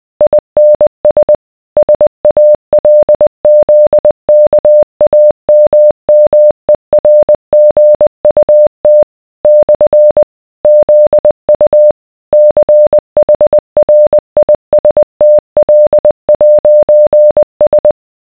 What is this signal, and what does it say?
Morse code, heard as a keyed tone, about 20 wpm, 600 Hz. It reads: INS SALZKAMMERGUT / ZU CHRISTL'S